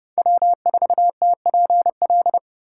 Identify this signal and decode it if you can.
Morse code, heard as a keyed tone, about 32 wpm, 700 Hz. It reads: W4TPL
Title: USA Call Signs - 5 Characters - 3x Character Spacing - 30wpm